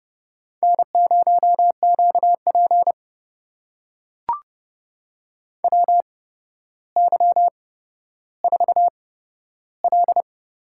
Morse code, heard as a keyed tone, about 32 wpm, spaced as for 13 wpm, 700 Hz. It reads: N0QP WY4L
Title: USA Call Signs - 4 Characters - 8x Character Spacing - 30wpm